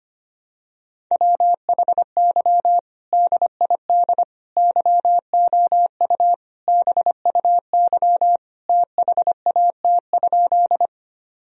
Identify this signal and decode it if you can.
Morse code, heard as a keyed tone, about 25 wpm, 700 Hz. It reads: WHY DID YOU BUY THAT?
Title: Sentences from Top 1000 Words - Review Part 4 - 25wpm